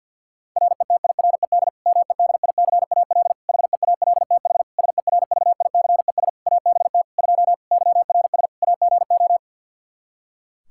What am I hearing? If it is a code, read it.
Morse code, heard as a keyed tone, about 52 wpm, 700 Hz. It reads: RETIRED MEDICAL HEALTH SERVICES ABT 2 YRS AGO
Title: Practice QSO - Exchange 3 - More Info - 50wpm